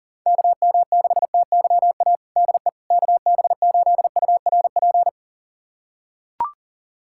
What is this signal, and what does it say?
Morse code, heard as a keyed tone, about 40 wpm, 700 Hz. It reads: KM6TYA DE KB8URP
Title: USA Callsign DE Callsign 40wpm